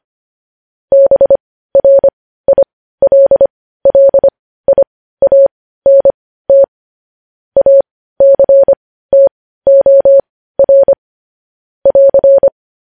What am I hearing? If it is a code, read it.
Morse code, heard as a keyed tone, about 25 wpm, spaced as for 15 wpm, 550 Hz. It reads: BRILLIANT ACTOR <AR>
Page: news headlines in morse code at 15 wpm